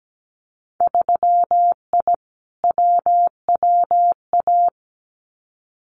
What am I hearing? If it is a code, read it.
Morse code, heard as a keyed tone, about 17 wpm, 700 Hz. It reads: 3I WWA
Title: Winter Field Day Exchanges Encore 17wpm